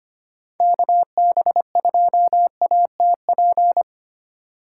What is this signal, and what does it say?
Morse code, heard as a keyed tone, about 25 wpm, 700 Hz. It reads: KB2ATP
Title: USA Call Signs - 3x Character Spacing - 25wpm